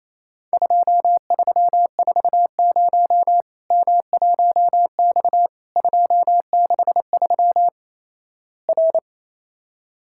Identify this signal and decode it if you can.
Morse code, heard as a keyed tone, about 28 wpm, 700 Hz. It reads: 2340 M1X 263 R
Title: Worked All Europe DX Contest - QTC Exchange - 28wpm